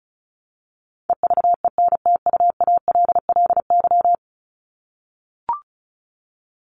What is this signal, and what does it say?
Morse code, heard as a keyed tone, about 34 wpm, 700 Hz. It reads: EVENTUALLY
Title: LY Suffix - Top 100 Words with Base Words 35wpm